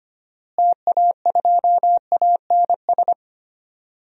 Morse code, heard as a keyed tone, about 25 wpm, 700 Hz. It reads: TA2ANS
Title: DXCC-100 Call Signs - 6 Characters - Repeated 3x - 25wpm